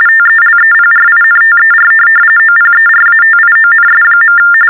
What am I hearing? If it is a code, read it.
TTY/TDD text telephone tones, about 45.45 baud. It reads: TELEPRINTER OPERATIONS.
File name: teletype-RTTY-45